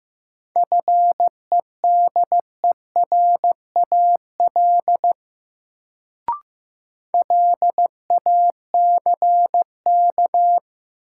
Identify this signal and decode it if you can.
Morse code, heard as a keyed tone, about 15 wpm, 700 Hz. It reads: FEDERAL LACK
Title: Sets of 3 Words - New Word Drill - 15wpm